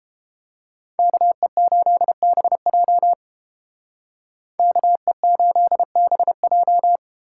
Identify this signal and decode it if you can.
Morse code, heard as a keyed tone, about 30 wpm, 700 Hz. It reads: KE8BJ KE8BJ